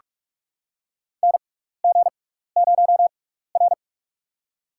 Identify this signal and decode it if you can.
Morse code, heard as a keyed tone, about 46 wpm, spaced as for 15 wpm, 700 Hz. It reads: NG0R
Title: USA Call Signs - 4 Characters - 6x Character Spacing - 45wpm